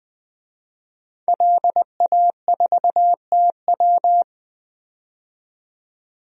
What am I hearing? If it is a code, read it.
Morse code, heard as a keyed tone, about 20 wpm, 700 Hz. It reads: LA4TW
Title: DXCC-100 Call Signs - 5 Characters - 20wpm